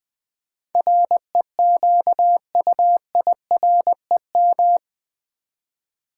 Morse code, heard as a keyed tone, about 20 wpm, 700 Hz. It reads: REQUIREM
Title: Build A Word - 4 Letters At A Time - 11-Letter Words - 20wpm